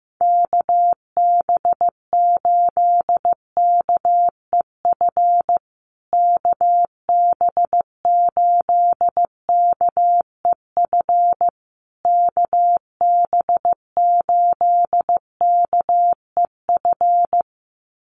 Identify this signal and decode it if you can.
Morse code, heard as a keyed tone, about 15 wpm, 700 Hz. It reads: KB8KEF KB8KEF KB8KEF
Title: USA Call Signs Repeated 3x 15wpm